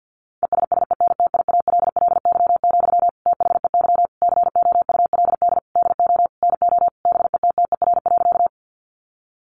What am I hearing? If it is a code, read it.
Morse code, heard as a keyed tone, about 50 wpm, 700 Hz. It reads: ESSENTIALLY, THEY COULD DO NO BETTER.
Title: Sentences from Top 1300 Words 50wpm